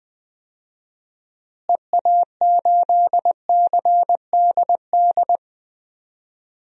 Morse code, heard as a keyed tone, about 20 wpm, 700 Hz. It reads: EA8CDD
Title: DXCC-100 Call Signs - 6 Characters - Repeated 3x - 20wpm